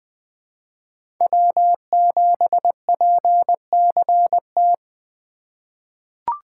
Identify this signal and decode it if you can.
Morse code, heard as a keyed tone, about 20 wpm, 700 Hz. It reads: W7PCT
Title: 2023 POTA Activator Call Signs - 20wpm